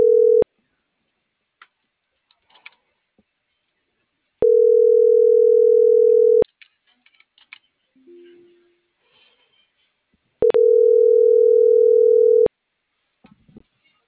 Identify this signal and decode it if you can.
Ringback tone (North America), 440 + 480 Hz.